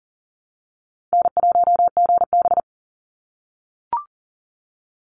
Morse code, heard as a keyed tone, about 40 wpm, 700 Hz. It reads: N1GB
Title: NA SOTA Chaser Callsigns 40wpm